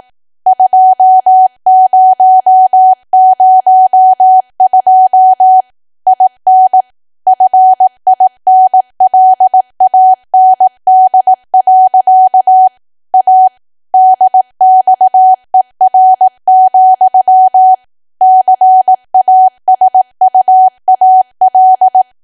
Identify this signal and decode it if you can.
Morse code, heard as a keyed tone, about 18 wpm, 750 Hz. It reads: 2002 IN FINLAND. A DXER, CASUAL
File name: Archive / 18 WPM / 18WPM